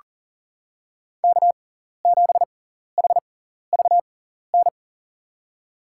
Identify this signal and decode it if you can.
Morse code, heard as a keyed tone, about 40 wpm, spaced as for 14 wpm, 700 Hz. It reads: K7HVN
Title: USA Call Signs - 5 Characters - 6x Character Spacing - 40wpm